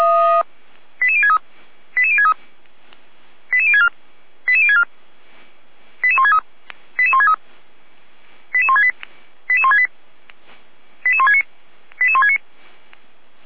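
Five-tone selective call (ZVEI), heard as quick runs of five tones, tones of about 80 ms. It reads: 88063 88063 88064 88064 88163 88163 88168 88168 88169 88169
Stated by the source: ZVEI five-tone selective call received on SDR.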